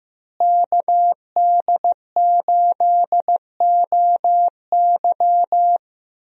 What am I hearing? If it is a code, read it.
Morse code, heard as a keyed tone, about 15 wpm, 700 Hz. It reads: KD8OY
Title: USA Call Signs - 5 Characters - Repeated 3x - 15wpm